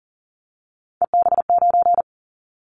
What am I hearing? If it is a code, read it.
Morse code, heard as a keyed tone, about 41 wpm, 700 Hz. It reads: EB9
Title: Callsign Prefixes for Top 100 DXCC - 40wpm